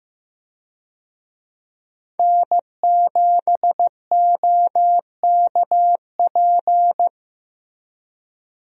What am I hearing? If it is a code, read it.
Morse code, heard as a keyed tone, about 15 wpm, 700 Hz. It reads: N7OKP
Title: USA Call Signs - 5 Characters - Repeated 2x - 15wpm